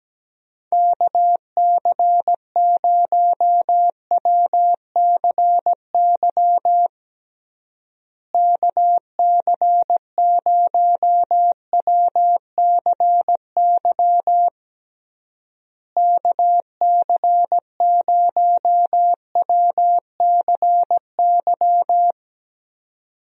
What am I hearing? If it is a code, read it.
Morse code, heard as a keyed tone, about 17 wpm, 700 Hz. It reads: KC0WCY KC0WCY KC0WCY
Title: USA Call Signs - 6 Characters - Repeated 3x - 17wpm